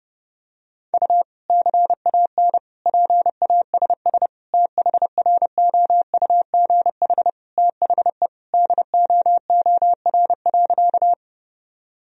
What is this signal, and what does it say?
Morse code, heard as a keyed tone, about 32 wpm, 700 Hz. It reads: U CAN PASS THROUGH THE DOOR.
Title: Sentences from Top 400 Words and Abbreviations 30wpm